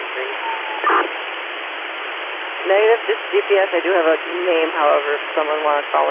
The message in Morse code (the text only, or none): none